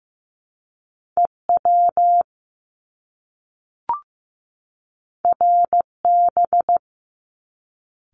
Morse code, heard as a keyed tone, about 15 wpm, 700 Hz. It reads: EW RB
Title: Top Two Letter Combinations in Words 15wpm